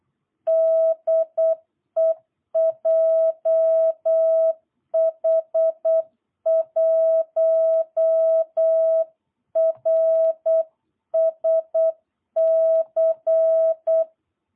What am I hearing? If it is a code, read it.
Morse code, heard as a keyed tone, about 8 wpm, 660 Hz. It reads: DE JH1RSC
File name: jh8cbh / 2022cw